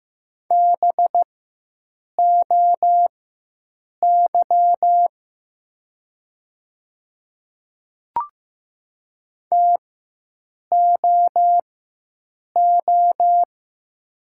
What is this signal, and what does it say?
Morse code, heard as a keyed tone, about 15 wpm, spaced as for 7 wpm, 700 Hz. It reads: BOY TOO
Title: 4x Character Spacing - Top 300 Words - 15wpm